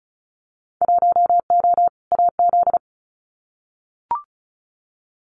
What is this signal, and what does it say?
Morse code, heard as a keyed tone, about 36 wpm, 690 Hz. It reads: 1O AZ
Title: Winter Field Day Exchanges 35wpm